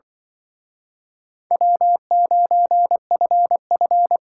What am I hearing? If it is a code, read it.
Morse code, heard as a keyed tone, about 23 wpm, 700 Hz. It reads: W9FF
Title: Speed Racing - USA Call Signs 20wpm